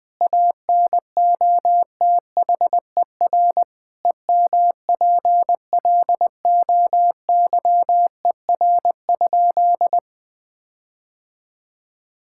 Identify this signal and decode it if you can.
Morse code, heard as a keyed tone, about 20 wpm, 700 Hz. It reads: ANOTHER EMPLOYER?
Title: Sentences from Top 1600 Words 20wpm